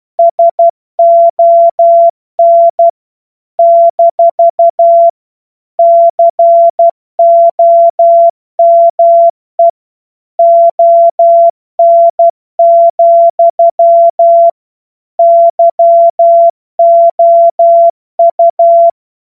Morse code, heard as a keyed tone, about 12 wpm, 680 Hz. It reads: SON - COME ON, YOU
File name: Morsekurs / Witze